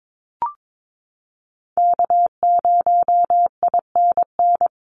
Morse code, heard as a keyed tone, about 22 wpm, 700 Hz. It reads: K0INN
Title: Call Signs - 22wpm